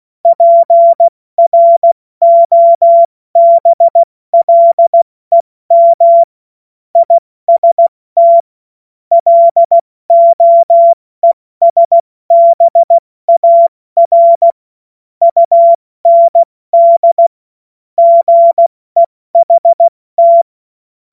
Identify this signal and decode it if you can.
Morse code, heard as a keyed tone, about 16 wpm, 680 Hz. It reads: PROBLEM IST LOESBAR UND GEHT
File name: Morsekurs / Witze / Witz 14. Juni 2022 70 F80